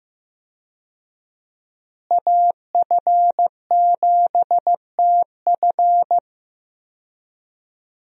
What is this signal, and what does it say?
Morse code, heard as a keyed tone, about 15 wpm, 700 Hz. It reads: AF7TF
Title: USA Call Signs - 5 Characters - Repeated 2x - 15wpm